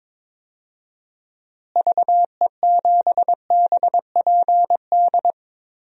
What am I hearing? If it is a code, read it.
Morse code, heard as a keyed tone, about 22 wpm, 700 Hz. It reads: VE7BPD
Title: DXCC-100 Call Signs - 6 Characters - Repeated 3x - 22wpm